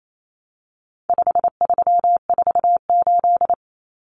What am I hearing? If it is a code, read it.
Morse code, heard as a keyed tone, about 28 wpm, 700 Hz. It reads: 5348
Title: Common Cut Numbers - 4-Digit 28wpm